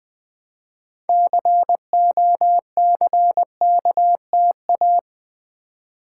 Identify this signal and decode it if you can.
Morse code, heard as a keyed tone, about 20 wpm, 700 Hz. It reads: COCKTA
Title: Build A Word - 2 Letters At A Time - 8-Letter Words - 20wpm